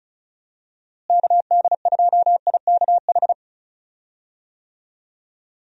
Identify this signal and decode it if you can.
Morse code, heard as a keyed tone, about 34 wpm, 700 Hz. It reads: KD2IKH
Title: USA Call Signs - 6 Characters - Repeated 2x - 35wpm